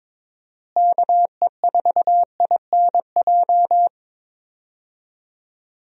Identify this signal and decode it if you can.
Morse code, heard as a keyed tone, about 22 wpm, 700 Hz. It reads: KE4INJ
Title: USA Call Signs - 6 Characters - Repeated 3x - 22wpm